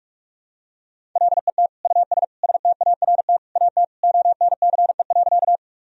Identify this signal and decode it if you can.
Morse code, heard as a keyed tone, about 46 wpm, 700 Hz. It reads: LET US START AT ONCE.
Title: Sentences from Top 1000 Words - Review Part 1 - 45wpm